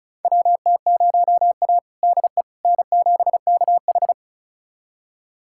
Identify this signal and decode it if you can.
Morse code, heard as a keyed tone, about 36 wpm, 700 Hz. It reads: WT0A DE N7KH
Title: USA Callsign DE Callsign - 4 Characters - 35wpm